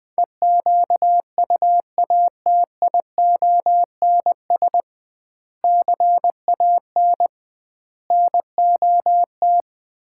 Morse code, heard as a keyed tone, about 21 wpm, 700 Hz. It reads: EQUATIONS CAN NOT
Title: 2x Word-Spacing - Sets of 5 Words - 20wpm